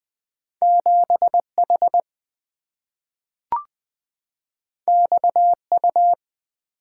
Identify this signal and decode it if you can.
Morse code, heard as a keyed tone, about 20 wpm, 700 Hz. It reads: 7 H X U
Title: Two Letter-Number Combinations - 20wpm